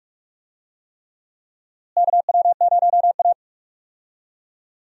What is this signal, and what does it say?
Morse code, heard as a keyed tone, about 46 wpm, 700 Hz. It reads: KW0A
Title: USA Call Signs - 4 Characters - Repeated 3x - 45wpm